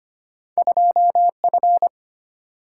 Morse code, heard as a keyed tone, about 26 wpm, 700 Hz. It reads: 2 F
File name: Two Letter-Number Combinations - 25wpm